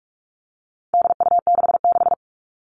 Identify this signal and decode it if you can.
Morse code, heard as a keyed tone, about 44 wpm, 700 Hz. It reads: DU66
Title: All Cut Numbers - 4-Digit 45wpm